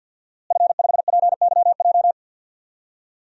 Morse code, heard as a keyed tone, about 45 wpm, 700 Hz. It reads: F5PYJ
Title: DXCC-100 Call Signs - 5 Characters - Repeated 2x - 50wpm